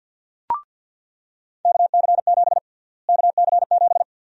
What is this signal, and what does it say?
Morse code, heard as a keyed tone, about 48 wpm, 700 Hz. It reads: KC7 KC7
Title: USA Call Sign Prefixes - Repeated 2x - 50wpm